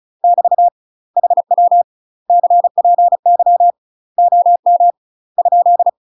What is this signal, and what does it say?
Morse code, heard as a keyed tone, about 36 wpm, 700 Hz. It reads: <BT> HW CPY OM ?
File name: UK MASTER QSOs TRAINING / 35WPM / UK MASTER QSOs 0041